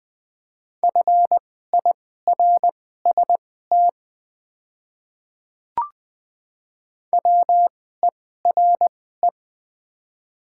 Morse code, heard as a keyed tone, about 20 wpm, spaced as for 14 wpm, 700 Hz. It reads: FIRST WERE
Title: 2x Character Spacing - Top 100 Words - 20wpm